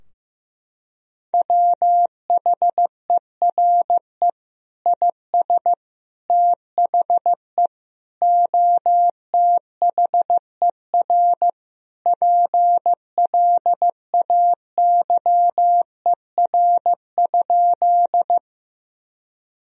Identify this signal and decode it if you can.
Morse code, heard as a keyed tone, about 15 wpm, 700 Hz. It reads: WHERE IS THE OTHER PLAYER?
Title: Sentences from Top 1000 Words - Review Part 4 - 15wpm